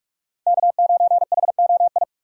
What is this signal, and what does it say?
Morse code, heard as a keyed tone, about 44 wpm, 700 Hz. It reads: K9HOI
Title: USA Call Signs - 5 Characters - 6x Character Spacing - 45wpm